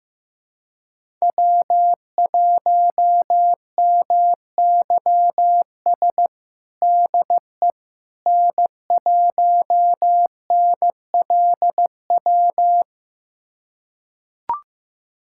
Morse code, heard as a keyed tone, about 15 wpm, 700 Hz. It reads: W1MYS DE N1NLW
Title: USA Callsign DE Callsign - 5 Characters - 15wpm